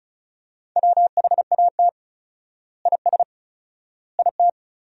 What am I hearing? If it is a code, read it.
Morse code, heard as a keyed tone, about 36 wpm, 700 Hz. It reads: WHAT IS IT
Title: 4x Word-Spacing - Sentences from Top 100 Words - 35wpm